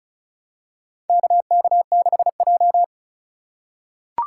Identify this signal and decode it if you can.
Morse code, heard as a keyed tone, about 36 wpm, 700 Hz. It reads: KK6J
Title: USA Call Signs - 4 Characters - 2x Character Spacing - 35wpm